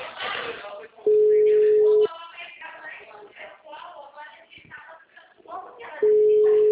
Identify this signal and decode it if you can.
Ringback tone (Europe, 425 Hz), with speech.
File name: out- -2002- - - .1564